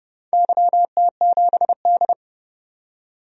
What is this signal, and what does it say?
Morse code, heard as a keyed tone, about 30 wpm, 700 Hz. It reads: YT7D
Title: DXCC-100 Call Signs - 4 Characters - 30wpm